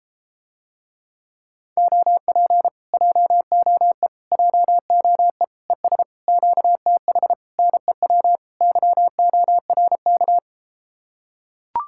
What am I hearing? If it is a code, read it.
Morse code, heard as a keyed tone, about 33 wpm, 700 Hz. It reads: OP JOE JOE ES QTH NEW YORK